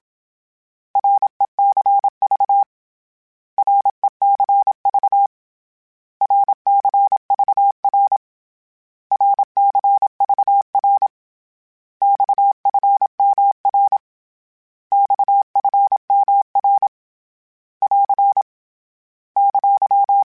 Morse code, heard as a keyed tone, about 27 wpm, 800 Hz. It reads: RECV RECV RCVR RCVR XFMR XFMR <AR> !
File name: Abreviations / Abreviations mp3s / Abreviations@25